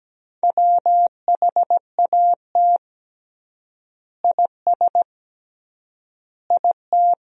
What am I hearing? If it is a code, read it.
Morse code, heard as a keyed tone, about 17 wpm, 700 Hz. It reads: WHAT IS IT
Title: 3x Word-Spacing - Sentences from Top 100 Words - 17wpm